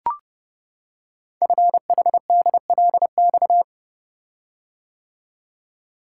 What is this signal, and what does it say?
Morse code, heard as a keyed tone, about 30 wpm, 700 Hz. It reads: FHDLX
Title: Code Groups - 5 Letter:Numbers - 30wpm